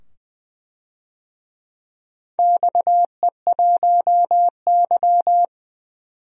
Morse code, heard as a keyed tone, about 21 wpm, 700 Hz. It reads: XE1Y
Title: DXCC-100 Call Signs - 4 Characters - 20wpm